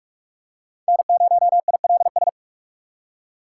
Morse code, heard as a keyed tone, about 46 wpm, 700 Hz. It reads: N0ILS
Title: USA Call Signs - 2x Character Spacing - 45wpm